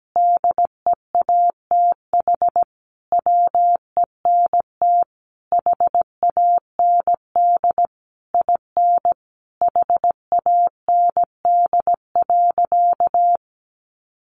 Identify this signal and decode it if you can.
Morse code, heard as a keyed tone, about 17 wpm, 700 Hz. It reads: DEATH WENT HAND IN HAND.
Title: Sentences from Top 1100 Words 17wpm